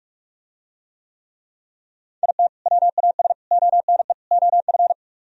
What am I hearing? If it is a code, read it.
Morse code, heard as a keyed tone, about 44 wpm, 700 Hz. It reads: IT WAS ONE OF